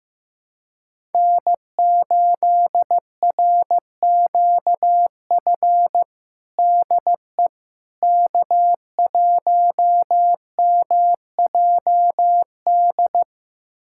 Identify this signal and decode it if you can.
Morse code, heard as a keyed tone, about 15 wpm, 700 Hz. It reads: N8RQF DE K1MJD
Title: USA Callsign DE Callsign - 5 Characters - 15wpm